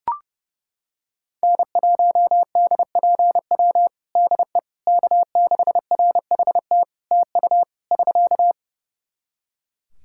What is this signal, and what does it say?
Morse code, heard as a keyed tone, about 30 wpm, 700 Hz. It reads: N1DPW DE K6RHT TU <SK>
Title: Practice QSO - Exchange 4 - Ending - 30wpm